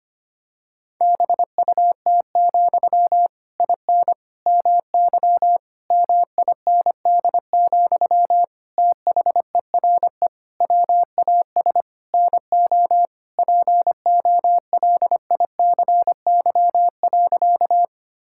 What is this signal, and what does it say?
Morse code, heard as a keyed tone, about 25 wpm, 700 Hz. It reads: BUT, IN MY MIND, THERE WAS NO POLICY.
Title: Sentences from Top 1000 Words - Review Part 5 - 25wpm